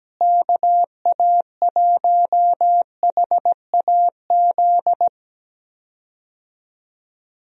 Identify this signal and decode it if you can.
Morse code, heard as a keyed tone, about 17 wpm, 700 Hz. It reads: KA1HAZ
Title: USA Call Signs - 6 Characters - Repeated 2x - 17wpm